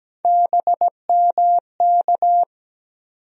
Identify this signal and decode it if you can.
Morse code, heard as a keyed tone, about 17 wpm, 700 Hz. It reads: BMK
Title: USA Call Sign Postfixes - 3 Letters 17wpm